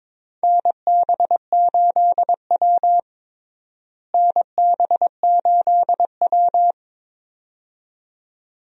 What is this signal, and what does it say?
Morse code, heard as a keyed tone, about 22 wpm, 700 Hz. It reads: NB8W NB8W
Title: USA Call Signs - 4 Characters - Repeated 2x - 22wpm